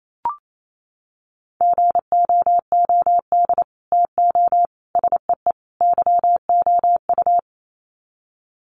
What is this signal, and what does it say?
Morse code, heard as a keyed tone, about 28 wpm, 700 Hz. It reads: GOOD TO SEE YOU
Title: Sets of 4 Words 28wpm